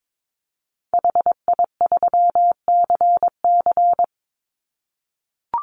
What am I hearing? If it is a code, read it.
Morse code, heard as a keyed tone, about 22 wpm, 700 Hz. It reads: HI3CC
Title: Call Signs - 22wpm